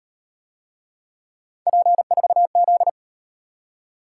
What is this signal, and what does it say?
Morse code, heard as a keyed tone, about 38 wpm, 700 Hz. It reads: P4Z